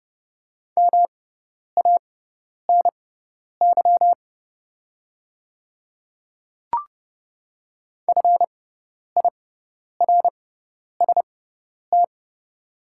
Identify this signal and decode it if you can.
Morse code, heard as a keyed tone, about 30 wpm, spaced as for 10 wpm, 700 Hz. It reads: MANY FIRST
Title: 6x Character Spacing - Top 100 Words - 30wpm